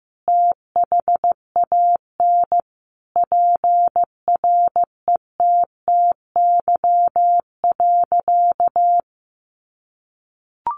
Morse code, heard as a keyed tone, about 15 wpm, 700 Hz. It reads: THAN PRETTY.
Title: Sentences from Top 1000 Words 15wpm